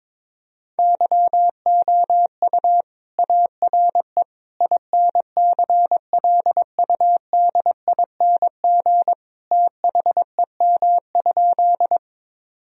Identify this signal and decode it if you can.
Morse code, heard as a keyed tone, about 22 wpm, 700 Hz. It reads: YOU ARE INCLUDING THEM?
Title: Sentences from Top 1000 Words - Review Part 4 - 22wpm